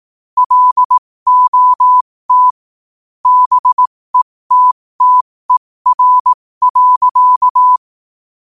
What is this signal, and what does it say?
Morse code, heard as a keyed tone, about 18 wpm, spaced as for 15 wpm, 1000 Hz. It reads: LOT BETTER.
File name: 15wpm